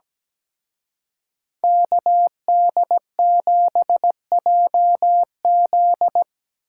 Morse code, heard as a keyed tone, about 19 wpm, 700 Hz. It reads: KD7JZ
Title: Speed Racing - USA Call Signs 17wpm